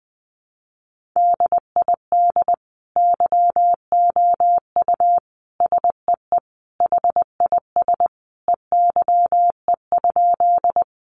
Morse code, heard as a keyed tone, about 20 wpm, 700 Hz. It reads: DID YOU SEE HIS EYE?
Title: Sentences from Top 300 Words 20wpm